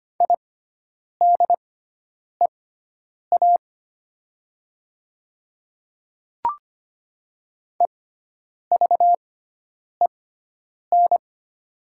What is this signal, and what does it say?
Morse code, heard as a keyed tone, about 25 wpm, spaced as for 9 wpm, 700 Hz. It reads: IDEA EVEN
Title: 6x Character Spacing - Top 500 Words - 25wpm